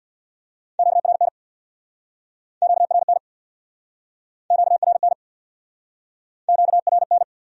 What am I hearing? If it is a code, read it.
Morse code, heard as a keyed tone, about 64 wpm, 700 Hz. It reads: QRN QRN QRN QRN